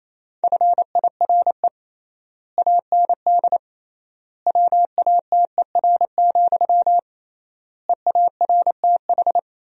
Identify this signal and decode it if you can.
Morse code, heard as a keyed tone, about 28 wpm, 700 Hz. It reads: FIRE AND WATER, EARTH
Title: 3x Word-Spacing - Sentences from Top 400 Words - 28wpm